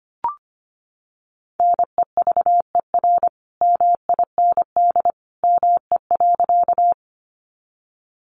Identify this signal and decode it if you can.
Morse code, heard as a keyed tone, about 25 wpm, 700 Hz. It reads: NEVER MIND ME.
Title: Sentences from Top 500 Words 25wpm